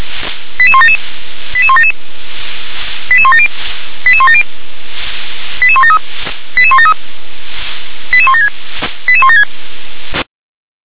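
Five-tone selective call (ZVEI), heard as quick runs of five tones, tones of about 70 ms. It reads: 88177 88177 88170 88170 88173 88173 88176 88176
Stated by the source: ZVEI five-tone selective call received via SDR.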